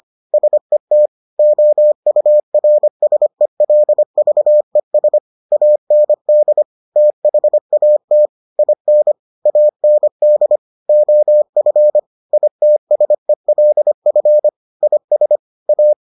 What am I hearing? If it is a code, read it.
Morse code, heard as a keyed tone, about 25 wpm, 600 Hz. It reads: SET OURSELVES AND THAT IN AND OF ITSELF IS A